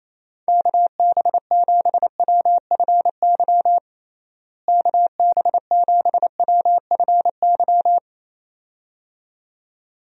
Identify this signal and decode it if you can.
Morse code, heard as a keyed tone, about 28 wpm, 700 Hz. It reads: KB7WFY KB7WFY
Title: USA Call Signs - 6 Characters - Repeated 2x - 28wpm